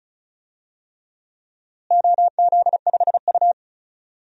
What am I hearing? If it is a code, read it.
Morse code, heard as a keyed tone, about 34 wpm, 700 Hz. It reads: OZ5U
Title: DXCC-100 Call Signs - 4 Characters - Repeated 3x - 35wpm